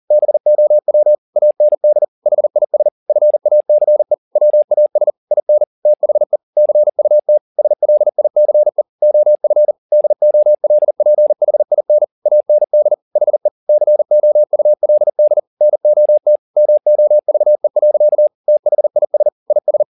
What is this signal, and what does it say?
Morse code, heard as a keyed tone, about 40 wpm, 600 Hz. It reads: BOW AND HIS FACE WAS IN THE CUT SLICE OF DOLPHIN AND HE COULD NOT MOVE. THIS IS
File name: the old man and the sea 2 40wpm